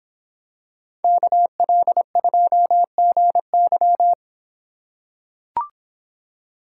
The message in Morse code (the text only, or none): KL2GY